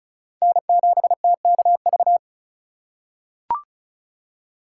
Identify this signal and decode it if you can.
Morse code, heard as a keyed tone, about 34 wpm, 700 Hz. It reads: N7TKV
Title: USA Call Signs - 4x Character Spacing - 35wpm